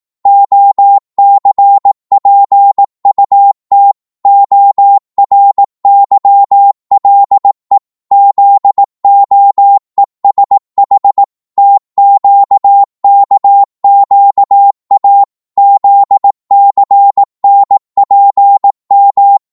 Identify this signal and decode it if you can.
Morse code, heard as a keyed tone, about 18 wpm, 800 Hz. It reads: OCPUT ORYLE ZOESH TQKQA ZCNPM